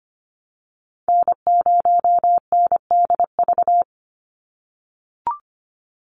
Morse code, heard as a keyed tone, about 25 wpm, 700 Hz. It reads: N0NDV
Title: USA Call Signs 25wpm